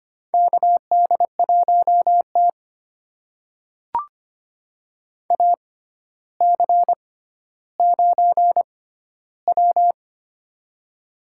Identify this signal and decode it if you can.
Morse code, heard as a keyed tone, about 25 wpm, spaced as for 13 wpm, 700 Hz. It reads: KD1T AC9W
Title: USA Call Signs - 4 Characters - 6x Character Spacing - 25wpm